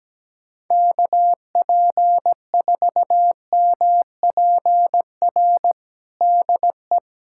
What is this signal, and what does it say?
Morse code, heard as a keyed tone, about 17 wpm, 700 Hz. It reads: KP4MPR DE
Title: USA Callsign DE Callsign - 6 Characters - 17wpm